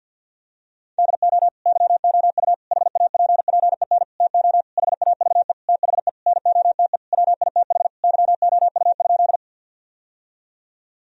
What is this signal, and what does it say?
Morse code, heard as a keyed tone, about 53 wpm, 700 Hz. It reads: DO YOU HAPPEN TO HAVE THE NOTE WITH YOU?
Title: Sentences from Top 1000 Words - Review Part 2 - 50wpm